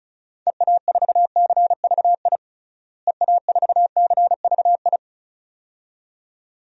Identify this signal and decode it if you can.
Morse code, heard as a keyed tone, about 36 wpm, 700 Hz. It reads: EA4CVI EA4CVI
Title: DXCC-100 Call Signs - 6 Characters - Repeated 2x - 35wpm